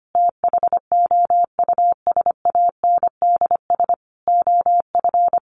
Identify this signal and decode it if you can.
Morse code, heard as a keyed tone, about 25 wpm, 700 Hz. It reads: THOUSANDS OF
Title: A Sound Of Thunder 25wpm